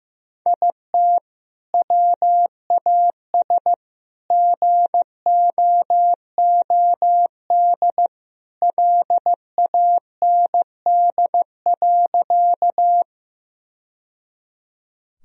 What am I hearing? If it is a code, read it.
Morse code, heard as a keyed tone, about 16 wpm, 700 Hz. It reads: IT WAS GOOD LAND.
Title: Sentences from Top 1000 Words - Review Part 1 - 15wpm